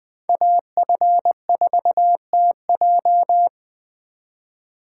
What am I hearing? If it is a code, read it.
Morse code, heard as a keyed tone, about 24 wpm, 700 Hz. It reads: AF4TJ
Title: Speed Racing - USA 2x2 Call Signs 20wpm